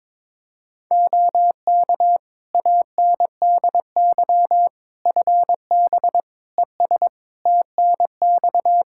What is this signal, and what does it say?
Morse code, heard as a keyed tone, about 22 wpm, 700 Hz. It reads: OK ANDY FB ES TNX
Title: K1USN SST QSO Practice 22wpm